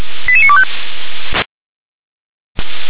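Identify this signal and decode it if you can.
Five-tone selective call (ZVEI), heard as quick runs of five tones, tones of about 70 ms. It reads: 88925
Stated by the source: ZVEI five-tone selective call received with an SDR.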